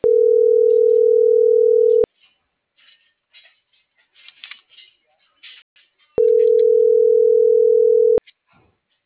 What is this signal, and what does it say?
Ringback tone (North America), 440 + 480 Hz.